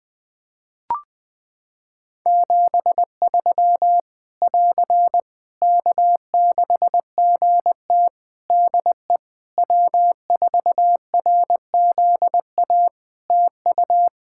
Morse code, heard as a keyed tone, about 20 wpm, 700 Hz. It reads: 73 <AR> K6GT DE W4RZA TU
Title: Practice QSO - Exchange 4 - Ending - 20wpm